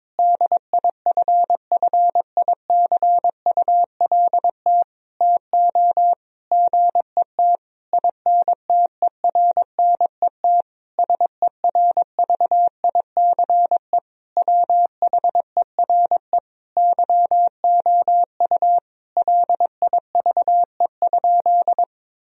Morse code, heard as a keyed tone, about 22 wpm, 700 Hz. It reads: DIFFICULT TO GET INTERNET SERVICE WHERE YOU LIVE?
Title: Sentences from Top 1000 Words - Review Part 4 - 22wpm